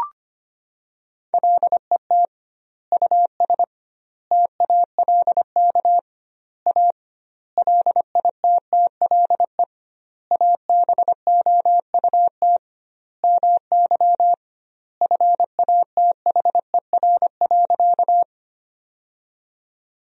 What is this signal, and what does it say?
Morse code, heard as a keyed tone, about 25 wpm, 700 Hz. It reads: LET US TALK A LITTLE ABOUT MY FATHER.
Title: 2x Word-Spacing - Sentences from Top 400 Words - 25wpm